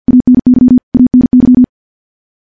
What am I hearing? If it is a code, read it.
Morse code, heard as a keyed tone, about 50 wpm, 250 Hz. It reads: ANY ANY
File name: morse / morse-training-50WPM